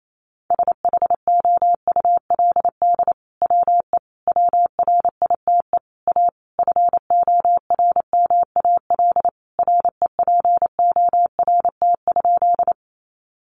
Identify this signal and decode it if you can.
Morse code, heard as a keyed tone, about 28 wpm, 700 Hz. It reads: SHOULD WE WRITE A FORMAL REPORT?
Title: Sentences from Top 1300 Words 28wpm